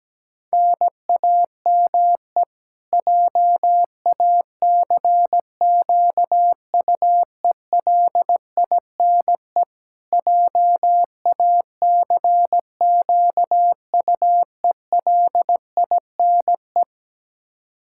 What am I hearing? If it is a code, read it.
Morse code, heard as a keyed tone, about 17 wpm, 700 Hz. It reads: NAME JACQUELINE JACQUELINE
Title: K1USN SST QSO Practice 17wpm